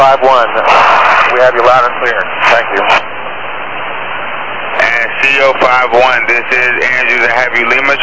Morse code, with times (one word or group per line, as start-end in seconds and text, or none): none